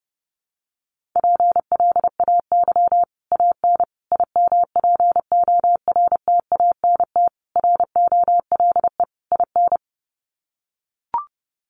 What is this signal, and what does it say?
Morse code, heard as a keyed tone, about 30 wpm, 700 Hz. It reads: PLAY AN IMPORTANT ROLE IN
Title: Sets of 5 Words Encore 30wpm